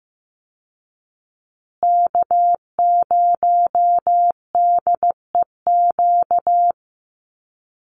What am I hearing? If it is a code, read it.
Morse code, heard as a keyed tone, about 15 wpm, 700 Hz. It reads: K0DEQ
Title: Call Signs Repeated 2x - 15wpm